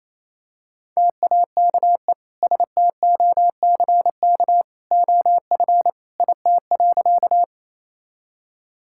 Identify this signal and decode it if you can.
Morse code, heard as a keyed tone, about 28 wpm, 700 Hz. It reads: TAKE STOCK OF IT.
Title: Sentences from Top 1000 Words - Review Part 5 - 28wpm